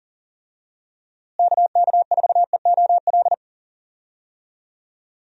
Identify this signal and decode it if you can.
Morse code, heard as a keyed tone, about 40 wpm, 700 Hz. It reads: KK4EOL
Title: USA Call Signs - 6 Characters - Repeated 3x - 40wpm